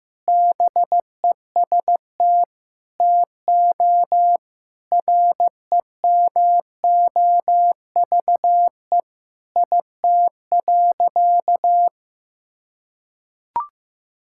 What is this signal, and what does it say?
Morse code, heard as a keyed tone, about 15 wpm, 700 Hz. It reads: BEST TO REMOVE IT.
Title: Sentences from Top 1000 Words - Review Part 5 - 15wpm